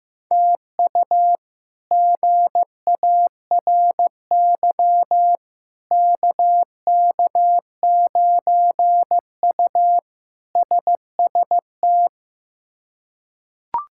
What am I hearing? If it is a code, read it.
Morse code, heard as a keyed tone, about 15 wpm, 700 Hz. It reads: TU GARY KK9U SST
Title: K1USN SST QSO Practice - 15wpm